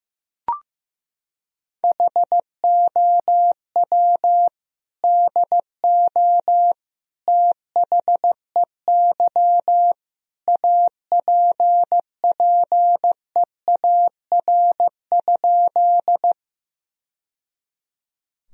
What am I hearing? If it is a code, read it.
Morse code, heard as a keyed tone, about 15 wpm, 700 Hz. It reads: HOW DO THEY APPEAR?
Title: Sentences from Top 1000 Words - Review Part 2 - 15wpm